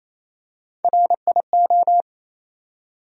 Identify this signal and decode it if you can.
Morse code, heard as a keyed tone, about 28 wpm, 700 Hz. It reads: RIO
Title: USA Call Sign Postfixes - 3 Letters 28wpm